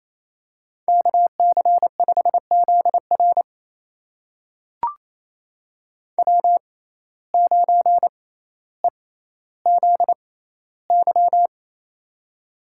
Morse code, heard as a keyed tone, about 28 wpm, spaced as for 14 wpm, 700 Hz. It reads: KC5ZR W9EZY
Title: USA Call Signs - 5 Characters - 6x Character Spacing - 28wpm